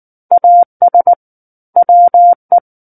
Morse code, heard as a keyed tone, about 19 wpm, 700 Hz. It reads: AS WE
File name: Book One / 10-Chapter 10 - 19 WPM